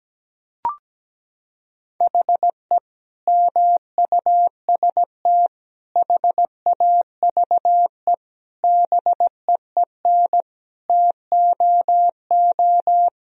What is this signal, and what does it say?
Morse code, heard as a keyed tone, about 17 wpm, 700 Hz. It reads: HE MUST HAVE BEEN TOO
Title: Sentences from Top 1000 Words - Review Part 3 - 17wpm